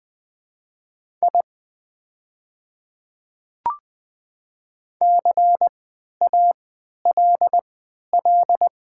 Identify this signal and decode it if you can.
Morse code, heard as a keyed tone, about 20 wpm, spaced as for 11 wpm, 700 Hz. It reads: I CALL
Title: 3x Character Spacing - Top 200 Words - 20wpm